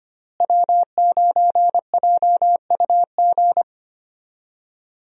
Morse code, heard as a keyed tone, about 25 wpm, 700 Hz. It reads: W9JUG
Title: USA Call Signs - 5 Characters - Repeated 3x - 25wpm